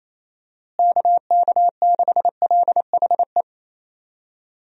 Morse code, heard as a keyed tone, about 28 wpm, 700 Hz. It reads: KK6LHE
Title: USA Call Signs - 6 Characters - Repeated 2x - 28wpm